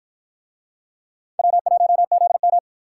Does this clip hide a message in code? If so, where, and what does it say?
Yes: Morse code from 1.4 to 2.6 s, heard as a keyed tone, about 53 wpm, 700 Hz. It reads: W1ZM